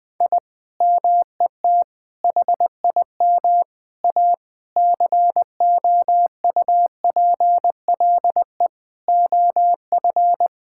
Morse code, heard as a keyed tone, about 21 wpm, 700 Hz. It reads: I MET HIM A COUPLE OF
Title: Sentences with 5-Word Phrases - 20wpm